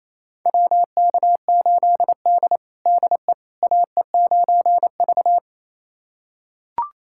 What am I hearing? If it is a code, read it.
Morse code, heard as a keyed tone, about 27 wpm, 700 Hz. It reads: WK8D DE AE9V
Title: USA Callsign DE Callsign - 4 Characters - 28wpm